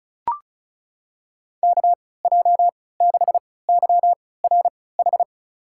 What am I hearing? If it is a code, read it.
Morse code, heard as a keyed tone, about 36 wpm, spaced as for 20 wpm, 700 Hz. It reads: KJ6YRH
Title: USA Call Signs - 6 Characters - 3x Character Spacing - 35wpm